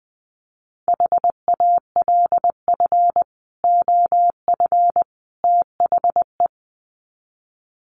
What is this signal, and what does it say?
Morse code, heard as a keyed tone, about 20 wpm, 700 Hz. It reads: HALF OF THE
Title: Sets of 3 Words 20wpm